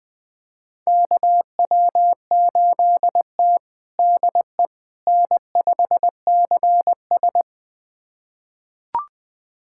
Morse code, heard as a keyed tone, about 20 wpm, 700 Hz. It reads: KW8T DE N5CS
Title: USA Callsign DE Callsign - 4 Characters - 20wpm